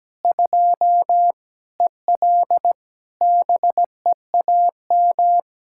Morse code, heard as a keyed tone, about 17 wpm, 700 Hz. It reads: 2 EL BEAM
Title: Practice QSO - Exchange 2 - Info - 17wpm